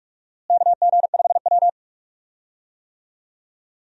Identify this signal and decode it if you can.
Morse code, heard as a keyed tone, about 44 wpm, 700 Hz. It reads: KG5W
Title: USA Call Signs - 4 Characters - Repeated 2x - 45wpm